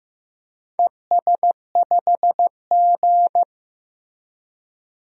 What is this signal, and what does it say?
Morse code, heard as a keyed tone, about 15 wpm, 700 Hz. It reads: ES5G
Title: DXCC-100 Call Signs - 4 Characters - 15wpm